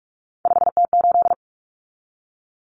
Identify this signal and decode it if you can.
Morse code, heard as a keyed tone, about 44 wpm, 700 Hz. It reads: 5T8
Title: Common Cut Numbers - 3-Digit 45wpm